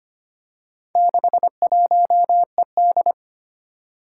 Morse code, heard as a keyed tone, about 25 wpm, 700 Hz. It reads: 61ED
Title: Four Challenging Letter-Number Combinations 25wpm